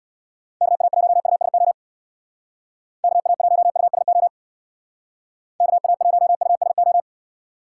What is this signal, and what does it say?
Morse code, heard as a keyed tone, about 65 wpm, 700 Hz. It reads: KA1USO KA1USO KA1USO